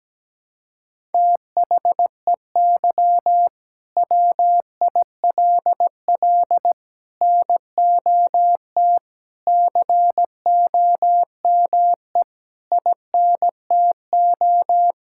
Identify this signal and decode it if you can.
Morse code, heard as a keyed tone, about 17 wpm, 700 Hz. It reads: THEY WILL NOT COME INTO
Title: Sentences from Top 1000 Words - Review Part 5 - 17wpm